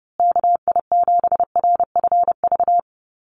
Morse code, heard as a keyed tone, about 30 wpm, 700 Hz. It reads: KI7RFV
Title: USA Call Signs Repeated 2x 30wpm